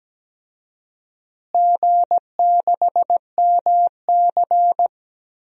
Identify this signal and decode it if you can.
Morse code, heard as a keyed tone, about 17 wpm, 700 Hz. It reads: G6MC
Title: DXCC-100 Call Signs - 4 Characters - Repeated 3x - 17wpm